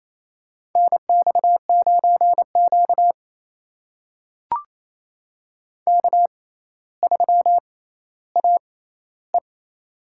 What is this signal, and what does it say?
Morse code, heard as a keyed tone, about 27 wpm, spaced as for 14 wpm, 700 Hz. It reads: NX9Q K3AE
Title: USA Call Signs - 4 Characters - 6x Character Spacing - 28wpm